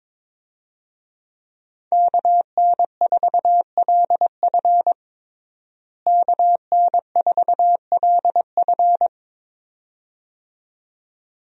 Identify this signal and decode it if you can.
Morse code, heard as a keyed tone, about 22 wpm, 700 Hz. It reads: KN4LF KN4LF
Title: USA Call Signs - 5 Characters - Repeated 2x - 22wpm